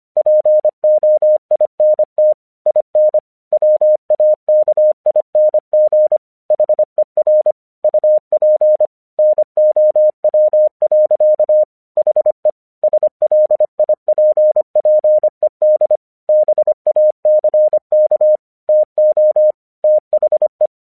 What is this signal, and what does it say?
Morse code, heard as a keyed tone, about 25 wpm, 610 Hz. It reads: POINT IN WAKING HER UP NOW. HE SLIPPED BACK TO THE
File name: 1 English 25wpm / AQC28